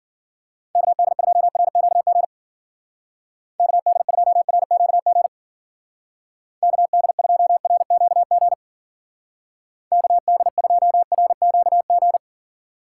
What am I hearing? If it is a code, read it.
Morse code, heard as a keyed tone, about 48 wpm, 700 Hz. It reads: KD2RQG KD2RQG KD2RQG KD2RQG
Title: Speed Racing - USA 2x3 Call Signs 40wpm